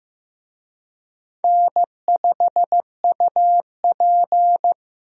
Morse code, heard as a keyed tone, about 15 wpm, 700 Hz. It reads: N5UP
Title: USA Call Signs - 4 Characters - 15wpm